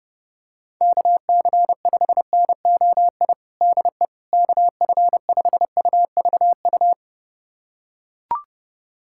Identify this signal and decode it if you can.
Morse code, heard as a keyed tone, about 30 wpm, 700 Hz. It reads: KC5NOI DE KF5UVU
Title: USA Callsign DE Callsign - 6 Characters - 30wpm